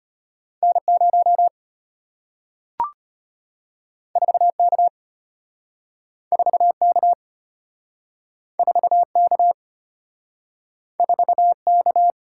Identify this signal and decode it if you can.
Morse code, heard as a keyed tone, about 34 wpm, 700 Hz. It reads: N0 4K 4K 4K 4K